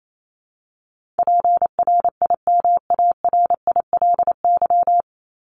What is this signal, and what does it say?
Morse code, heard as a keyed tone, about 28 wpm, 700 Hz. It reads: PRIMARILY
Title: LY Suffix - Top 100 Words with Base Words 28wpm